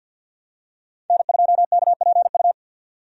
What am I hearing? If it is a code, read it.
Morse code, heard as a keyed tone, about 49 wpm, 700 Hz. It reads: N2KPU
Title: USA Call Signs - 5 Characters - 50wpm